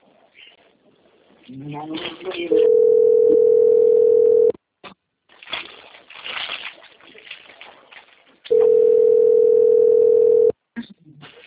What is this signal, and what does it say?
Ringback tone (North America), 440 + 480 Hz, with speech.
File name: rg-1600-unknown- - - .15374